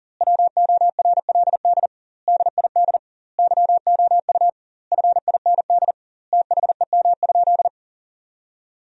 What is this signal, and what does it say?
Morse code, heard as a keyed tone, about 40 wpm, 700 Hz. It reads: WORLD DID YOU FIND THEM?
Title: 2x Word-Spacing - Sentences from Top 200 Words - 40wpm